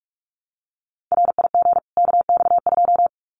Morse code, heard as a keyed tone, about 44 wpm, 700 Hz. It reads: RIG KX2
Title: Common QSO Phrases 45wpm